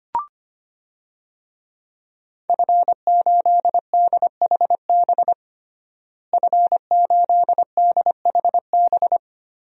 Morse code, heard as a keyed tone, about 25 wpm, 700 Hz. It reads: F8DHB F8DHB
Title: DXCC-100 Call Signs - 5 Characters - Repeated 3x - 25wpm